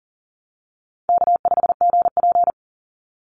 Morse code, heard as a keyed tone, about 40 wpm, 700 Hz. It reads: K5GP
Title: Easy Call Signs - 40wpm